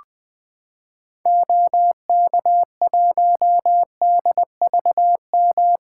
Morse code, heard as a keyed tone, about 21 wpm, 700 Hz. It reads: OK1DVM
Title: 2023 SOTA Activator Call Signs - 20wpm